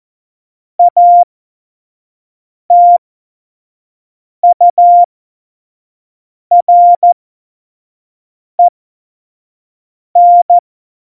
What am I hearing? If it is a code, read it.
Morse code, heard as a keyed tone, about 14 wpm, spaced as for 5 wpm, 700 Hz. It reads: ATUREN
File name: Morsekurs / Witze / Witz 15. Januar 2019, 25 F70